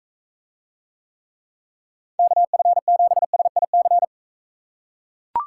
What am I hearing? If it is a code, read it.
Morse code, heard as a keyed tone, about 42 wpm, 700 Hz. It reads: KF7SIC